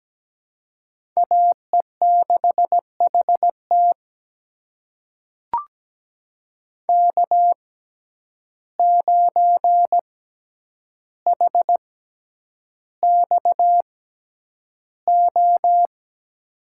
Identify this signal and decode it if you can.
Morse code, heard as a keyed tone, about 17 wpm, spaced as for 9 wpm, 700 Hz. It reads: AE6HT K9HXO
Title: USA Call Signs - 6x Character Spacing - 17wpm